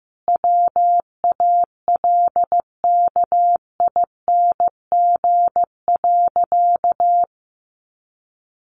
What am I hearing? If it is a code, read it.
Morse code, heard as a keyed tone, about 15 wpm, 700 Hz. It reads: WALKING.
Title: Easy Phrases 15wpm